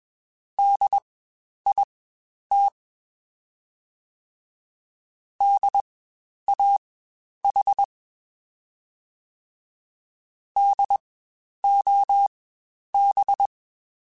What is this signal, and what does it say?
Morse code, heard as a keyed tone, about 21 wpm, spaced as for 10 wpm, 800 Hz. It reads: DIT DAH DOB
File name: Spaced Alphas / Spaced Alphas mp3s / Ds sp @20